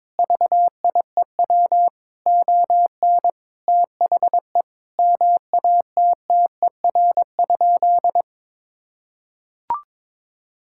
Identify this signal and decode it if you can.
Morse code, heard as a keyed tone, about 22 wpm, 700 Hz. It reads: VIEW ON THE MATTER?
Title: Sentences from Top 1000 Words - Review Part 6 - 22wpm